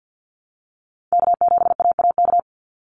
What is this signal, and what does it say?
Morse code, heard as a keyed tone, about 50 wpm, 700 Hz. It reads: K7AAK
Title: USA Call Signs 50wpm